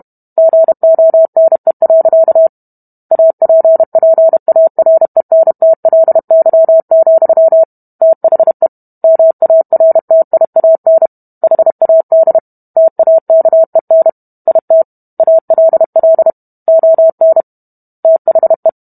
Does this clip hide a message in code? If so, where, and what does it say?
Yes: Morse code from 0.4 to 18.7 s, heard as a keyed tone, about 32 wpm, 650 Hz. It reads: GONE. APPARENTLY, THE MARTIAN HAD TAKEN IT ALL ON THE